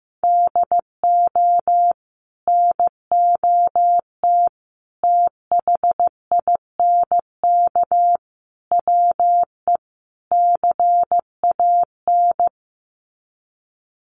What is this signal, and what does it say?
Morse code, heard as a keyed tone, about 15 wpm, 700 Hz. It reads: DO NOT THINK WE CAN
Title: Sets of 5 Words Encore2 15wpm